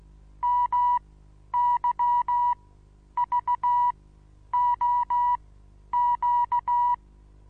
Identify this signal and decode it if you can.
Morse code, heard as a keyed tone, about 16 wpm, spaced as for 10 wpm, 970 Hz. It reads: MYVOQ
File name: Onlinelehrgang / morsen / DJ4UF Morsen25